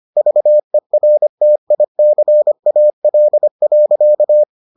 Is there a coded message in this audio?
Yes: Morse code from 0.2 to 4.4 s, heard as a keyed tone, about 25 wpm, 600 Hz. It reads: VERTICAL.